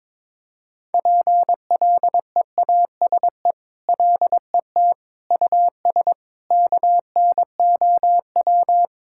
Morse code, heard as a keyed tone, about 22 wpm, 700 Hz. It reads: PLEASE LET US KNOW
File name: practice / 22wpm / Sentences with 8-Word Phrases - 22wpm